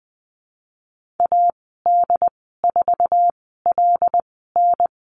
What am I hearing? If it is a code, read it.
Morse code, heard as a keyed tone, about 20 wpm, spaced as for 14 wpm, 700 Hz. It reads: AD4LN
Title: 076 Lesson 13g - Letter L - Call Signs 20wpm